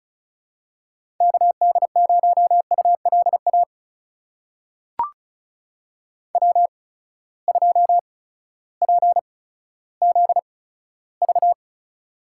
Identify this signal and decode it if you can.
Morse code, heard as a keyed tone, about 36 wpm, spaced as for 16 wpm, 700 Hz. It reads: KD0ULA W2PZV
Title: USA Call Signs - 8x Character Spacing - 35wpm